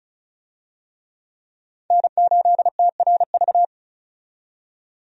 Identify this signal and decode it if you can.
Morse code, heard as a keyed tone, about 34 wpm, 700 Hz. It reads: N8TRV
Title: USA Call Signs - 5 Characters - 35wpm